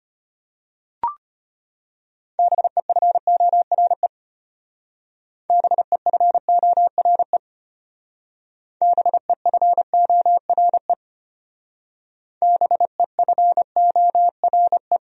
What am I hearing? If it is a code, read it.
Morse code, heard as a keyed tone, about 32 wpm, 700 Hz. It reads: BEFORE BEFORE BEFORE BEFORE